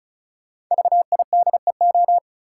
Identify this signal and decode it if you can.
Morse code, heard as a keyed tone, about 36 wpm, 700 Hz. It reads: VIDEO
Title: Sentences from Top 1000 Words - Review Part 4 - New Word Drill - 35wpm